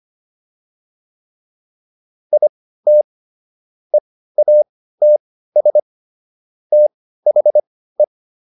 Morse code, heard as a keyed tone, about 25 wpm, spaced as for 15 wpm, 600 Hz. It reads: IT EATS THE